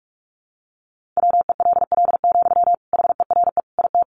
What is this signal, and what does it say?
Morse code, heard as a keyed tone, about 44 wpm, 700 Hz. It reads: WELL, HERE IT
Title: Sentences from Top 1400 Words 45wpm